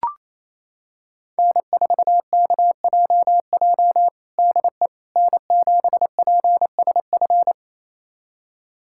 Morse code, heard as a keyed tone, about 28 wpm, 700 Hz. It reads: N4KJJ DE N7PSF
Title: USA Callsign DE Callsign - 5 Characters - 28wpm